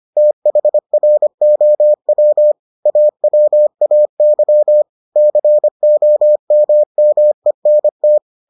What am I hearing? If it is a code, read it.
Morse code, heard as a keyed tone, about 25 wpm, 600 Hz. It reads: THROW AWAY COMMENT